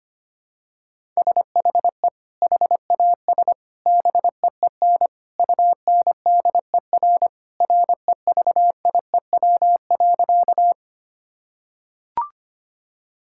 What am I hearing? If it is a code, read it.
Morse code, heard as a keyed tone, about 25 wpm, 700 Hz. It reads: SHE HAS BEEN UNDER REVIEW.
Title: Sentences from Top 1000 Words - Review Part 6 - 25wpm